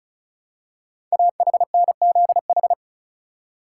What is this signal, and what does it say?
Morse code, heard as a keyed tone, about 36 wpm, 690 Hz. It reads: AHNZH
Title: Code Groups - 5 Letters - 35wpm